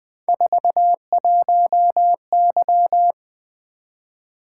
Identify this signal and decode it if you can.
Morse code, heard as a keyed tone, about 20 wpm, 700 Hz. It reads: 41Y
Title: Three Letter-Number Combinations - 20wpm